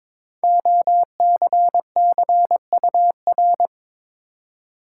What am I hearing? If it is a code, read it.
Morse code, heard as a keyed tone, about 22 wpm, 700 Hz. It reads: OCCUR
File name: 5 Letter Words - First 3 Letters Given - 22wpm